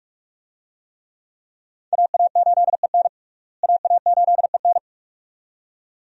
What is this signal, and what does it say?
Morse code, heard as a keyed tone, about 42 wpm, 700 Hz. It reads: AA8EN AA8EN
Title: USA Call Signs - 5 Characters - Repeated 2x - 45wpm